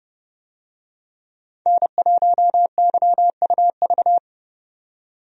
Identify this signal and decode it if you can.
Morse code, heard as a keyed tone, about 29 wpm, 700 Hz. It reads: N1YUV
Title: USA Call Signs - 5 Characters - 30wpm